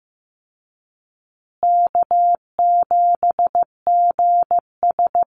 Morse code, heard as a keyed tone, about 15 wpm, 700 Hz. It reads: K7GS
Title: Easy Call Signs Repeated 2x - 15wpm